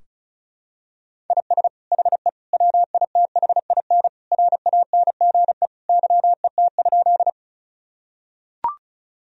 Morse code, heard as a keyed tone, about 36 wpm, 700 Hz. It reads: IS HE WITHIN RANGE YET?
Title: Sentences from Top 1000 Words - Review Part 4 - 35wpm